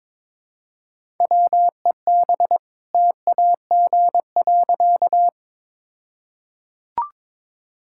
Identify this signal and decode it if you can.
Morse code, heard as a keyed tone, about 22 wpm, 690 Hz. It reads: WEB TAG.
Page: morse-code-ninja-practice-22wpm